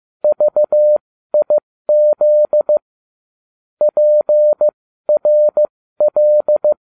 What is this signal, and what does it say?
Morse code, heard as a keyed tone, about 15 wpm, spaced as for 13 wpm, 600 Hz. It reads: VIZ PRL